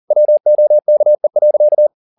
Morse code, heard as a keyed tone, about 40 wpm, 600 Hz. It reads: WOKE.
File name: the old man and the sea 2 40wpm